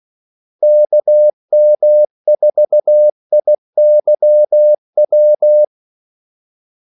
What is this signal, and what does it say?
Morse code, heard as a keyed tone, about 16 wpm, 600 Hz. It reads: KM4IYW
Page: Kaldesignaler 16 wpm